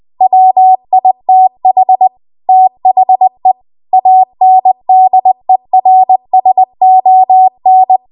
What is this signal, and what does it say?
Morse code, heard as a keyed tone, about 20 wpm, 750 Hz. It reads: WITH THE ANDERSON
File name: Morse / 20WPM